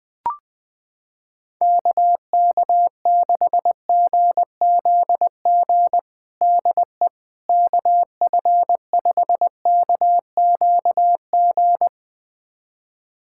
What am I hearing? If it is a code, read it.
Morse code, heard as a keyed tone, about 20 wpm, 700 Hz. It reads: KK6GZG DE KF5KQG
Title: USA Callsign DE Callsign - 6 Characters - 20wpm